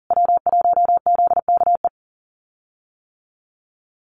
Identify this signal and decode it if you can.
Morse code, heard as a keyed tone, about 40 wpm, 700 Hz. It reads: W1ZKE
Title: USA Call Signs Repeated 3x 40wpm